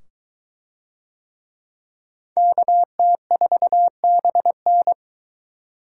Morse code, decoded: KT4BN

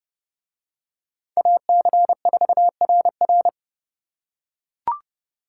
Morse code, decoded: AC4RR

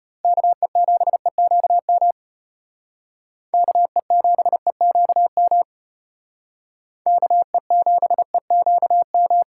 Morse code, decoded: KE7EQM KE7EQM KE7EQM

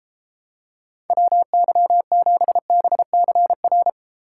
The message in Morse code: WY7BCR

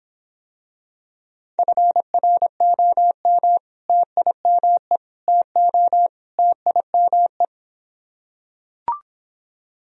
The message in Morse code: FROM TIME TO TIME

